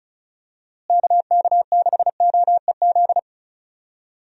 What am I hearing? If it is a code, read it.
Morse code, heard as a keyed tone, about 34 wpm, 700 Hz. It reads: KK6OEZ
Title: USA Call Signs - 6 Characters - Repeated 3x - 35wpm